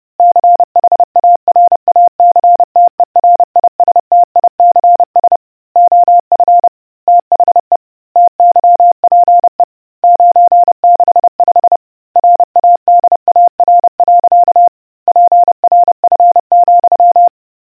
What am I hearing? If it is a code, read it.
Morse code, heard as a keyed tone, about 30 wpm, 700 Hz. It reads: CHARACTERISTICS OF THE TYPE 965 RADAR. PRF,